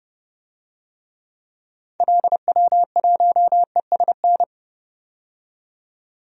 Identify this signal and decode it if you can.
Morse code, heard as a keyed tone, about 30 wpm, 700 Hz. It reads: LW1ESN
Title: DXCC-100 Call Signs - 6 Characters - 30wpm